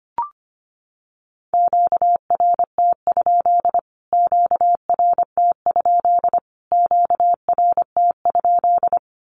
Morse code, heard as a keyed tone, about 25 wpm, 700 Hz. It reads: QRT? QRT? QRT?
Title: Q-Codes Repeated 3x 25wpm